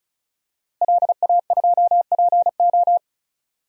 Morse code, heard as a keyed tone, about 36 wpm, 700 Hz. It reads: LA2PO
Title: DXCC-100 Call Signs - 5 Characters - Repeated 3x - 35wpm